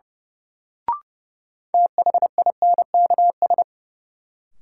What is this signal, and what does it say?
Morse code, heard as a keyed tone, about 30 wpm, 700 Hz. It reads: THINKS
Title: Sentences from Top 1000 Words - Review Part 6 - New Word Drill - 30wpm